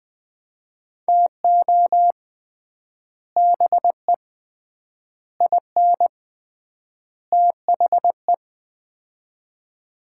Morse code, heard as a keyed tone, about 20 wpm, 700 Hz. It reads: TO BE IN THE
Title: 3x Word-Spacing - Sets of 4 Words - 20wpm